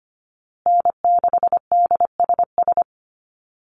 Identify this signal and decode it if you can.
Morse code, heard as a keyed tone, about 25 wpm, 700 Hz. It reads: N6DSS
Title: USA Call Signs Repeated 2x 25wpm